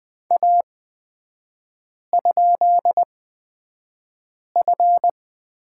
Morse code, heard as a keyed tone, about 20 wpm, spaced as for 5 wpm, 700 Hz. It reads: A?F
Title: 250 Lesson 40e - BK - Three Characters 20wpm